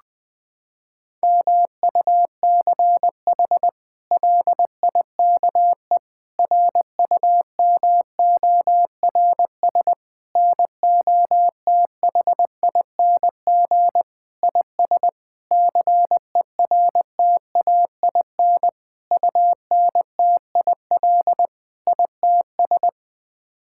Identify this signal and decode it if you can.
Morse code, heard as a keyed tone, about 20 wpm, 700 Hz. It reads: MUCH LIKE RUMORS NOTHING IS CERTAIN UNTIL ITS
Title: Sets of 8 Words - 20wpm